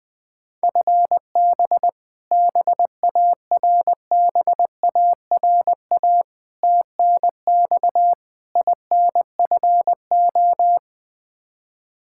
Morse code, heard as a keyed tone, about 20 wpm, 700 Hz. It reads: FB BARBARA TNX INFO
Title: Practice QSO - Exchange 3 - More Info - 20wpm